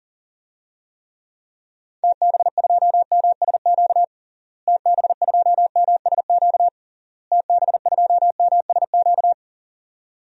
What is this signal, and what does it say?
Morse code, heard as a keyed tone, about 40 wpm, 700 Hz. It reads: TB2MSQ TB2MSQ TB2MSQ
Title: DXCC-100 Call Signs - 6 Characters - Repeated 3x - 40wpm